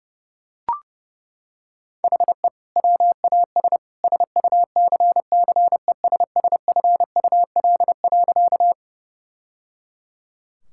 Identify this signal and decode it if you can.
Morse code, heard as a keyed tone, about 30 wpm, 700 Hz. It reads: HE WAS SUCCESSFUL.
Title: Sentences from Top 1000 Words - Review Part 6 - 30wpm